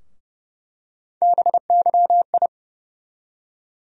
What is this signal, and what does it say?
Morse code, heard as a keyed tone, about 29 wpm, 700 Hz. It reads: BYI
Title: USA Call Sign Postfixes - 3 Letters 30wpm